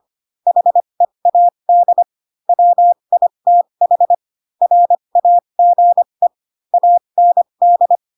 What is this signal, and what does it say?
Morse code, heard as a keyed tone, about 25 wpm, spaced as for 22 wpm, 700 Hz. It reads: HEAD WITH RAGE AND
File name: Pinocchio / Pinocchio ch2 18 25 ik1hge